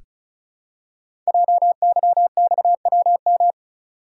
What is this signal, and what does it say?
Morse code, heard as a keyed tone, about 35 wpm, 690 Hz. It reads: JYXWM
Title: Code Groups - 5 Letters - 35wpm